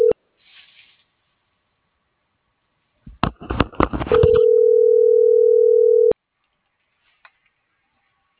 Ringback tone (North America), 440 + 480 Hz.